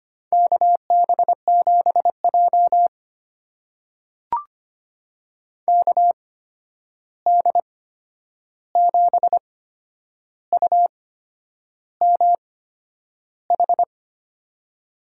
Morse code, heard as a keyed tone, about 25 wpm, spaced as for 9 wpm, 700 Hz. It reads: KB7J KD7UMH
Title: USA Call Signs - 8x Character Spacing - 25wpm